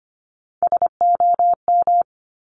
Morse code, heard as a keyed tone, about 25 wpm, 700 Hz. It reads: SOM
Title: Top Three Letter Combinations in Words 25wpm